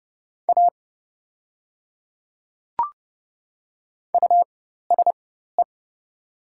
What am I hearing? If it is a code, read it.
Morse code, heard as a keyed tone, about 31 wpm, spaced as for 14 wpm, 700 Hz. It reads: A USE
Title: 4x Character Spacing - Top 200 Words - 30wpm